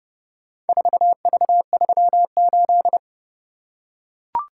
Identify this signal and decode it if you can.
Morse code, heard as a keyed tone, about 29 wpm, 700 Hz. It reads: 4V38